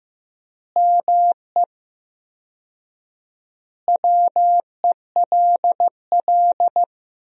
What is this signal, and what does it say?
Morse code, heard as a keyed tone, about 15 wpm, 700 Hz. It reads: ME WELL
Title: 4x Word-Spacing - Sentences from Top 400 Words - 15wpm